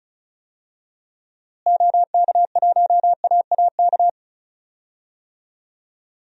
Morse code, heard as a keyed tone, about 36 wpm, 700 Hz. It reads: OK1AAK
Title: DXCC-100 Call Signs - 6 Characters - Repeated 2x - 35wpm